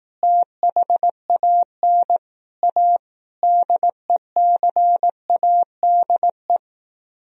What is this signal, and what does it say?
Morse code, heard as a keyed tone, about 18 wpm, 700 Hz. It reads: THAN A DECADE
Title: Speed-Racing - Sets of 5 Words 15wpm